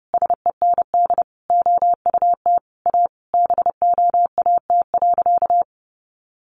Morse code, heard as a keyed tone, about 28 wpm, 700 Hz. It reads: SEND OUT A BOAT.
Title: Sentences from Top 1500 Words 30wpm